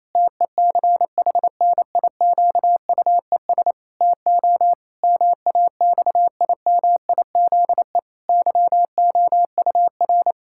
Morse code, heard as a keyed tone, about 28 wpm, 700 Hz. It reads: TECHNIQUES TO MAXIMIZE YOUR
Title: Pangram Challenge v2 - 28wpm